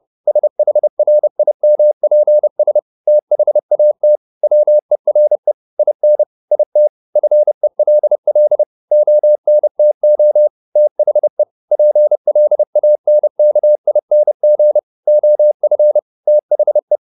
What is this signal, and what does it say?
Morse code, heard as a keyed tone, about 30 wpm, 600 Hz. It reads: SHRIMPS THAT WERE IN IT FELL ONTO THE PLANKING OF THE